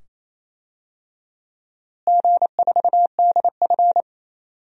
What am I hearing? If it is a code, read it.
Morse code, heard as a keyed tone, about 28 wpm, 700 Hz. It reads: G4DF
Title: DXCC-100 Call Signs - 4 Characters - Repeated 2x - 28wpm